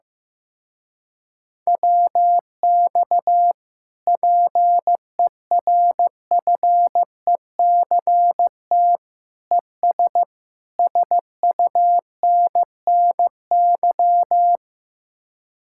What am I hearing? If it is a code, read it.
Morse code, heard as a keyed tone, about 15 wpm, 700 Hz. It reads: WX PERFECT ES SUNNY
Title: Speed-Racing Common QSO Phrases Encore - 15wpm